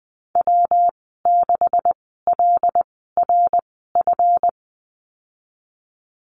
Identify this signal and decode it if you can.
Morse code, heard as a keyed tone, about 20 wpm, spaced as for 15 wpm, 700 Hz. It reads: W6LRF
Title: 140 Lesson 23g - Letter f - Call Signs 20wpm